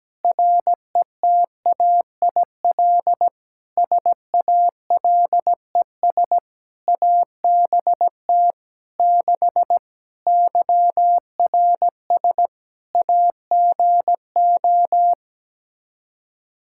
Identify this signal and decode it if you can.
Morse code, heard as a keyed tone, about 17 wpm, 700 Hz. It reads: RETAIL SALES ABT 6 YRS AGO
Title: Practice QSO - Exchange 3 - More Info - 17wpm